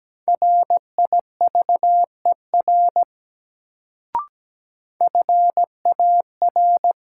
Morse code, heard as a keyed tone, about 17 wpm, 700 Hz. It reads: RIVER FAR
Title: Sentences from Top 300 Words - New Word Drill - 17wpm